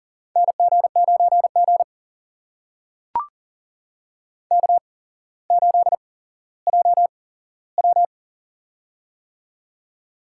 Morse code, heard as a keyed tone, about 38 wpm, spaced as for 17 wpm, 700 Hz. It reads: NG9G K8JW
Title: USA Call Signs - 4 Characters - 8x Character Spacing - 40wpm